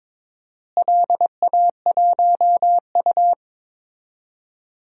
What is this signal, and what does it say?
Morse code, heard as a keyed tone, about 22 wpm, 700 Hz. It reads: LA1U
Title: DXCC-100 Call Signs - 4 Characters - Repeated 3x - 22wpm